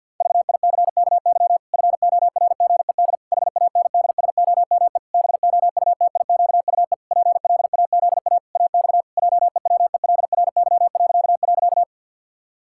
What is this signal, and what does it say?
Morse code, heard as a keyed tone, about 52 wpm, 700 Hz. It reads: VICKY FORGED HANDSOME BOUTIQUE PLAZA AX JEWELRY..
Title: Pangram Challenge v2 - 50wpm